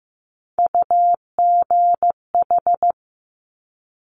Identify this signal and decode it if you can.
Morse code, heard as a keyed tone, about 15 wpm, 700 Hz. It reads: UGH
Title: Top Three Letter Combinations in Words 15wpm